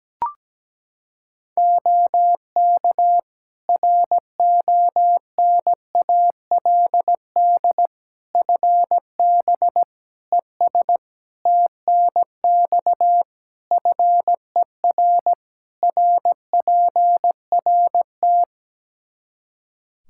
Morse code, heard as a keyed tone, about 17 wpm, 700 Hz. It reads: OK RONALD FB ES TNX FER RPRT
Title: Practice QSO - Exchange 2 - Info - 17wpm